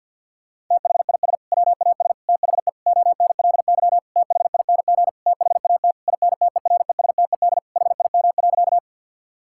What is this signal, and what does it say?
Morse code, heard as a keyed tone, about 50 wpm, 700 Hz. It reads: THIS WAS THE ONLY THING THAT INTERESTED HIM.
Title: Sentences from Top 1000 Words - Review Part 6 - 50wpm